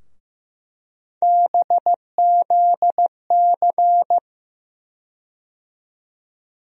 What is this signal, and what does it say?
Morse code, heard as a keyed tone, about 15 wpm, 700 Hz. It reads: BZC
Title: USA Call Sign Postfixes - 3 Letters 15wpm